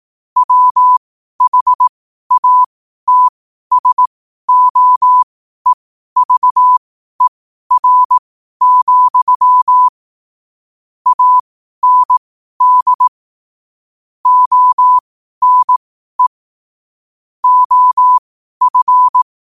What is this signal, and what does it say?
Morse code, heard as a keyed tone, about 18 wpm, spaced as for 12 wpm, 1000 Hz. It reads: WHATSOEVER, AND ONE OF
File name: qotdfiles / 12wpm